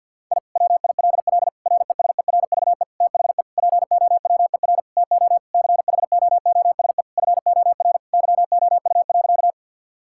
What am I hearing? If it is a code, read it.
Morse code, heard as a keyed tone, about 50 wpm, 700 Hz. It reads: I WILL RESERVE THE POWER TO CHOOSE FOR YOU.
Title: Sentences from Top 1000 Words - Review Part 6 - 50wpm